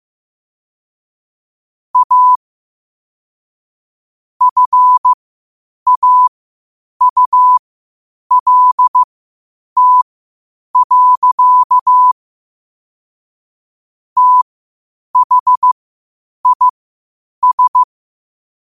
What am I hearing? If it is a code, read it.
Morse code, heard as a keyed tone, about 15 wpm, spaced as for 8 wpm, 1000 Hz. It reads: A FAULT. THIS
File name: qotdfiles / 8wpm